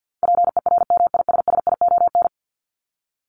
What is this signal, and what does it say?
Morse code, heard as a keyed tone, about 48 wpm, 700 Hz. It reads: PERMISSION
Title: Top 1501-1750 Words 50wpm